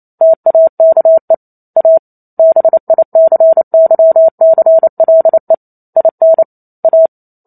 Morse code, heard as a keyed tone, about 28 wpm, 650 Hz. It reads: TAKE A BICYCLE IN A
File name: Chapter 2 - 28 WPM